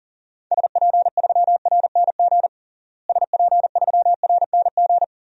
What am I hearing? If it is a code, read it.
Morse code, heard as a keyed tone, about 40 wpm, 700 Hz. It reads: SP3RNG SP3RNG
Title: DXCC-100 Call Signs - 6 Characters - Repeated 3x - 40wpm